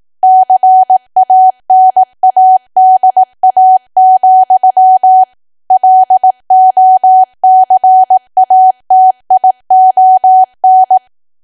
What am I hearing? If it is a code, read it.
Morse code, heard as a keyed tone, about 18 wpm, 750 Hz. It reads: CANADA, LOCATION
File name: cwa / W1AW-Level2 / 18WPM